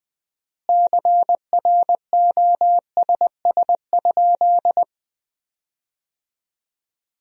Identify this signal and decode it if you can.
Morse code, heard as a keyed tone, about 21 wpm, 700 Hz. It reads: CROSS?
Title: 4x Word-Spacing - Sentences from Top 300 Words - 20wpm